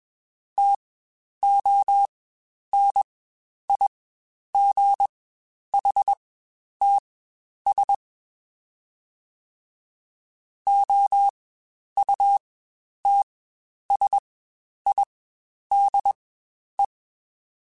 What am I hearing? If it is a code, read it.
Morse code, heard as a keyed tone, about 21 wpm, spaced as for 10 wpm, 800 Hz. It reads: TONIGHTS OUTSIDE
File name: Spaced Characters / Spaced Characters-mp3s / Weather 4 21 sp @20